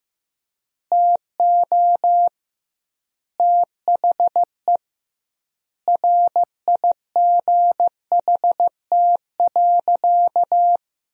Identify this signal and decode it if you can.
Morse code, heard as a keyed tone, about 16 wpm, 700 Hz. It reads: TO THE RIGHT.
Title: 2x Word-Spacing - Sentences from Top 300 Words - 15wpm